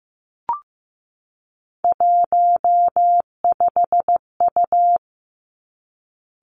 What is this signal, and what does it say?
Morse code, heard as a keyed tone, about 15 wpm, 700 Hz. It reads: 15U
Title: Three Challenging Letter-Number Combinations 15wpm